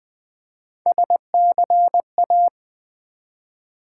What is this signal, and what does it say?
Morse code, heard as a keyed tone, about 20 wpm, 700 Hz. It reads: SCA
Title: Build A Word - 1 Letter At A Time - 3 to 5-Letter Words - 20wpm